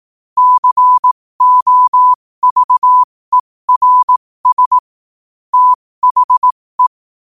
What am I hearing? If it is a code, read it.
Morse code, heard as a keyed tone, about 18 wpm, spaced as for 15 wpm, 1000 Hz. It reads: COVERS THE
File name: qotdfiles / 15wpm